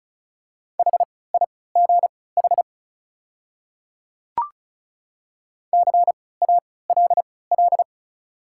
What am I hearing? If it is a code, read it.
Morse code, heard as a keyed tone, about 36 wpm, spaced as for 20 wpm, 690 Hz. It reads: HIGH CALL
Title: 3x Character Spacing - Top 200 Words - 35wpm